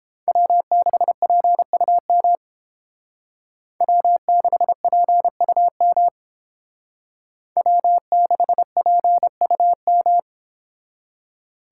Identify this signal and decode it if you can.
Morse code, heard as a keyed tone, about 31 wpm, 700 Hz. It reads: W6PUM W6PUM W6PUM